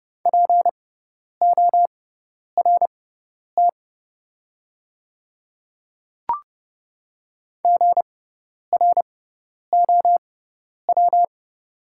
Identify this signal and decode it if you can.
Morse code, heard as a keyed tone, about 30 wpm, spaced as for 10 wpm, 700 Hz. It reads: PORT GROW
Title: 6x Character Spacing - Top 300 Words - 30wpm